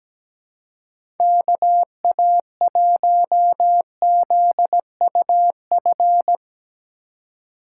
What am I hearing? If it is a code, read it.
Morse code, heard as a keyed tone, about 17 wpm, 700 Hz. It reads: KA1ZUF
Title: USA Call Signs - 6 Characters - 2x Character Spacing - 17wpm